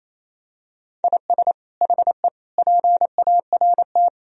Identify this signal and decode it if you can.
Morse code, heard as a keyed tone, about 28 wpm, 700 Hz. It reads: IS HE PART
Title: Sentences from Top 1000 Words - Review Part 2 - 28wpm